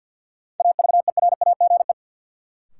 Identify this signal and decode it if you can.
Morse code, heard as a keyed tone, about 52 wpm, 700 Hz. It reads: AVERAGE
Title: Sets of 5 Words - New Word Drill - 50wpm